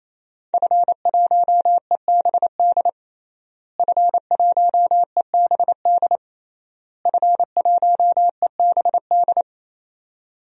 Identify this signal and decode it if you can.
Morse code, heard as a keyed tone, about 28 wpm, 700 Hz. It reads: F1EBD F1EBD F1EBD
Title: DXCC-100 Call Signs - 5 Characters - Repeated 3x - 28wpm